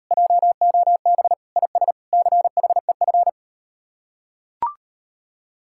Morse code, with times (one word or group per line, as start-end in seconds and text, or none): 0.1-1.3 JOB
1.6-1.9 IS
2.1-3.3 CHEF